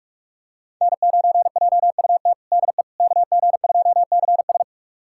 Morse code, heard as a keyed tone, about 46 wpm, 700 Hz. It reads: N9JUT DE KG2CS
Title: USA Callsign DE Callsign - 5 Characters - 45wpm